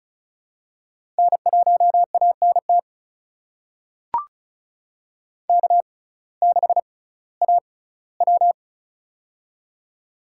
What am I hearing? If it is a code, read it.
Morse code, heard as a keyed tone, about 36 wpm, spaced as for 19 wpm, 700 Hz. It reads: N1ANT K6AW
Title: USA Call Signs - 6x Character Spacing - 35wpm